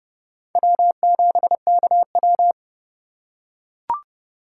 Morse code, heard as a keyed tone, about 28 wpm, 700 Hz. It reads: W7KW